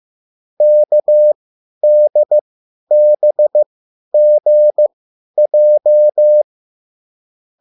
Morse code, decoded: KDBGJ

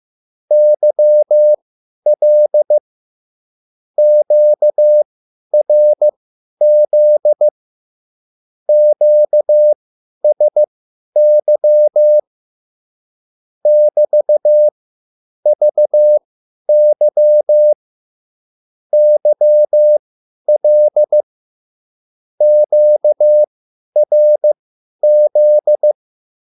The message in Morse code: YL QRZ QSY <BT>VY YL QRZ

